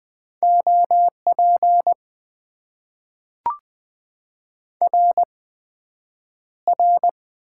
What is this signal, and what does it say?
Morse code, heard as a keyed tone, about 20 wpm, spaced as for 9 wpm, 700 Hz. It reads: OP RR
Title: Top 90 QSO Elements - 8x Character Spacing - 20wpm